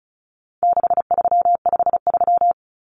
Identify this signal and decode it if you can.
Morse code, heard as a keyed tone, about 34 wpm, 700 Hz. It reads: 6353
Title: Numbers - 4-Digit 35wpm